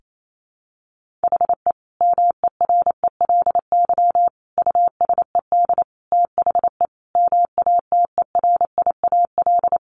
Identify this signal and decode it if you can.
Morse code, heard as a keyed tone, about 28 wpm, 700 Hz. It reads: HE MERELY USED THE MATERIAL
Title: Sentences from Top 1400 Words 28wpm